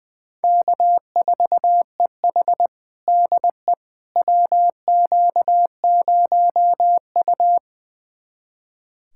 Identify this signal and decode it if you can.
Morse code, heard as a keyed tone, about 20 wpm, 700 Hz. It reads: K4EH DE WQ0U
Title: USA Callsign DE Callsign - 4 Characters - 20wpm